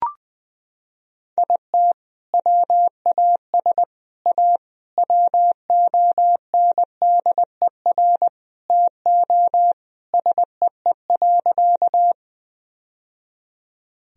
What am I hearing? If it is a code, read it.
Morse code, heard as a keyed tone, about 20 wpm, 700 Hz. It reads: IT WAS A WONDER TO SEE.
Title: Sentences from Top 1000 Words - Review Part 3 - 20wpm